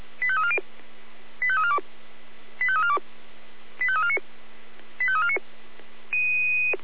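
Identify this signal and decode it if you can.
Five-tone selective call (ZVEI), heard as quick runs of five tones, tones of about 70 ms. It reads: 85340 85342 85342 85349 85349
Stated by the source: ZVEI five-tone call received by SDR.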